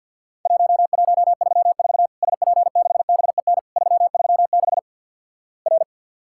Morse code, heard as a keyed tone, about 50 wpm, 700 Hz. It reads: 1134 SP6BEN 336 R
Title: Worked All Europe DX Contest - QTC Exchange with Repeats - 50wpm